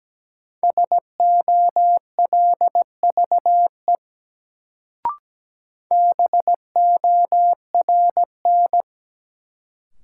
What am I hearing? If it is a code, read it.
Morse code, heard as a keyed tone, about 17 wpm, 700 Hz. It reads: SOLVE BORN
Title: Sentences from Top 1000 Words - New Word Drill - 17wpm